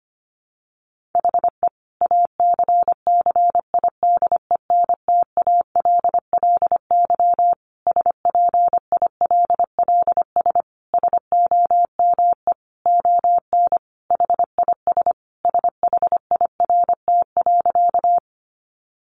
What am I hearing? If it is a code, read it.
Morse code, heard as a keyed tone, about 25 wpm, 700 Hz. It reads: HE ACCIDENTALLY SPILLS SOME ON HIS SHIRT.
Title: Additional Easy Phrases 25wpm